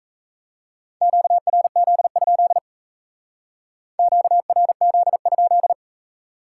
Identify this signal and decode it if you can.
Morse code, heard as a keyed tone, about 33 wpm, 700 Hz. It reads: QRZ? QRZ?
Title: Speed-Racing Common QSO Phrases Encore - 28wpm